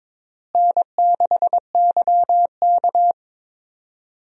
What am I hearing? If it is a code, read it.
Morse code, heard as a keyed tone, about 22 wpm, 700 Hz. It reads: N6YK
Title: USA Call Signs - 4 Characters - Repeated 3x - 22wpm